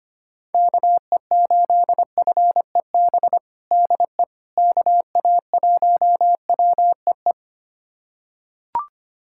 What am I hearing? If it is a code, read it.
Morse code, heard as a keyed tone, about 25 wpm, 700 Hz. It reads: KE8FEB DE KA1WEE
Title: USA Callsign DE Callsign - 6 Characters - 25wpm